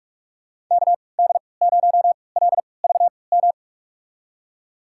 Morse code, heard as a keyed tone, about 46 wpm, spaced as for 26 wpm, 700 Hz. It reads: KD0LVM
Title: USA Call Signs - 6 Characters - 3x Character Spacing - 45wpm